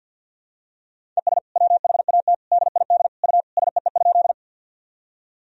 Morse code, heard as a keyed tone, about 47 wpm, 700 Hz. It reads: ES WHAT DID U SEE?
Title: Sentences from Top 100 Words and Abbreviations 50wpm